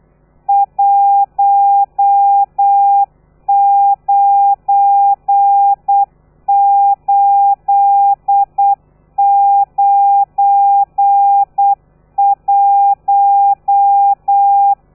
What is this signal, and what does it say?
Morse code, heard as a keyed tone, about 8 wpm, 800 Hz. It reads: 19891
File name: telegrafia / i2tfj / Lez12 35